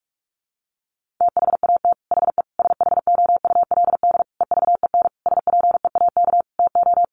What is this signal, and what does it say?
Morse code, heard as a keyed tone, about 44 wpm, 700 Hz. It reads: THAT HE SHOULD EVEN SPEAK TO
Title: Sentences from Top 1500 Words 45wpm